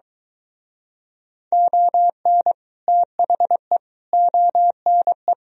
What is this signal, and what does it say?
Morse code, heard as a keyed tone, about 23 wpm, 700 Hz. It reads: ON THE ONE